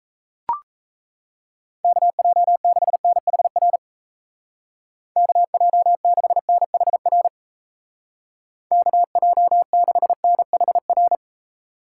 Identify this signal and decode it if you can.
Morse code, heard as a keyed tone, about 34 wpm, 700 Hz. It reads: KJ6NHR KJ6NHR KJ6NHR
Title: Speed Racing - USA 2x3 Call Signs 28wpm